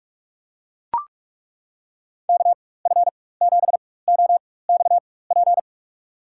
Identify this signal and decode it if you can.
Morse code, heard as a keyed tone, about 41 wpm, spaced as for 20 wpm, 700 Hz. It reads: KF7OXP
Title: USA Call Signs - 6 Characters - 4x Character Spacing - 45wpm